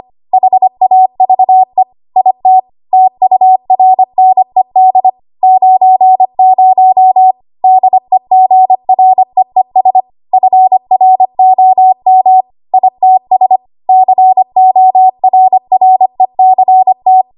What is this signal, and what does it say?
Morse code, heard as a keyed tone, about 25 wpm, 750 Hz. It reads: HAVE IT TURNED 90 DEGREES FROM ITS CORRECT
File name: Archive / 25 WPM / 25WPM